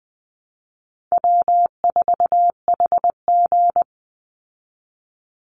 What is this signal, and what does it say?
Morse code, heard as a keyed tone, about 20 wpm, 700 Hz. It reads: W4HG
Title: Easy Call Signs Repeated 2x - 20wpm